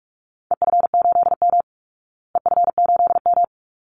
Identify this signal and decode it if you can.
Morse code, heard as a keyed tone, about 44 wpm, 700 Hz. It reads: EF8M EF8M
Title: Call Signs Repeated 3x - 45wpm